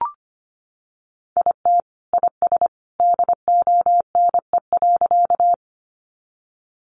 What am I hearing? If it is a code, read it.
Morse code, heard as a keyed tone, about 25 wpm, 700 Hz. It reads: IT IS DONE.
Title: Sentences from Top 500 Words 25wpm